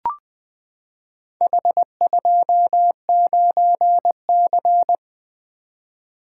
Morse code, heard as a keyed tone, about 20 wpm, 700 Hz. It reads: H29C
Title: Four Letter-Number Combinations - 20wpm